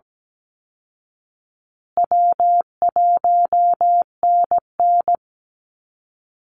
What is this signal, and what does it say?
Morse code, heard as a keyed tone, about 17 wpm, 700 Hz. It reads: W1NN
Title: Easy Call Signs Repeated 3x - 17wpm